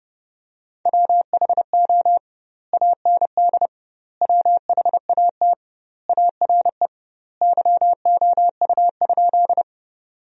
Morse code, heard as a keyed tone, about 32 wpm, 700 Hz. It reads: WHO AND WHAT ARE YOU?
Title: 2x Word-Spacing - Sentences from Top 100 Words - 30wpm